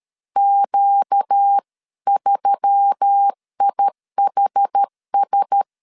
Morse code, heard as a keyed tone, about 13 wpm, 800 Hz. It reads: Q3IHS
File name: CW / DET / 1 6 1500C S A 12wpm 3440